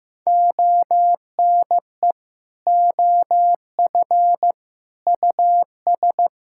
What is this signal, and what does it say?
Morse code, heard as a keyed tone, about 15 wpm, 700 Hz. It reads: ONE OF US
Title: Sentences from Top 1600 Words 15wpm